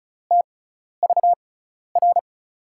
Morse code, heard as a keyed tone, about 36 wpm, spaced as for 12 wpm, 700 Hz. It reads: TVR
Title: USA Call Signs - 5 Characters - 6x Character Spacing - 35wpm